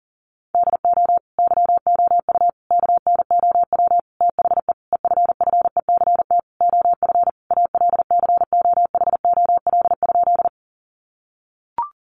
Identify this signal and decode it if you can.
Morse code, heard as a keyed tone, about 40 wpm, 700 Hz. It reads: DO YOU KNOW THE EFFECT OF ALCOHOL?
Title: Sentences from Top 1400 Words 40wpm